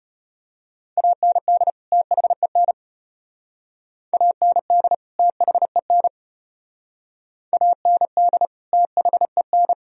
Morse code, heard as a keyed tone, about 34 wpm, 700 Hz. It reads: AND THEN AND THEN AND THEN